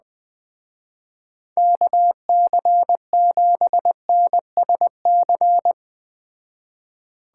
Morse code, heard as a keyed tone, about 20 wpm, 700 Hz. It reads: KC7NSC
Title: Speed Racing - USA 2x3 Call Signs 17wpm